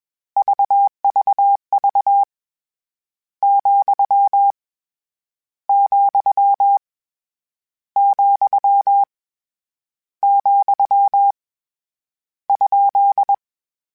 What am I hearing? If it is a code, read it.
Morse code, heard as a keyed tone, about 21 wpm, 800 Hz. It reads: VVV , , , , ?
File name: Misc / Misc-mp3s / Punct @20